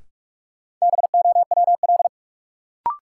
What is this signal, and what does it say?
Morse code, heard as a keyed tone, about 46 wpm, 700 Hz. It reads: BOWL
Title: Sentences from Top 1200 Words - New Word Drill - 45wpm